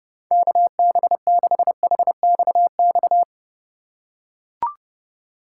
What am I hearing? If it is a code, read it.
Morse code, heard as a keyed tone, about 31 wpm, 700 Hz. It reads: KB6HXX
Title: USA Call Signs - 6 Characters - 4x Character Spacing - 30wpm